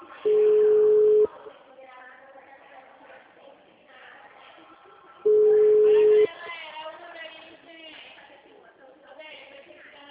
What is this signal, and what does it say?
Ringback tone (Europe, 425 Hz), with speech.